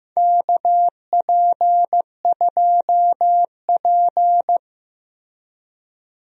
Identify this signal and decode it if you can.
Morse code, heard as a keyed tone, about 15 wpm, 700 Hz. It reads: KP2P
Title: DXCC-100 Call Signs - 4 Characters - 15wpm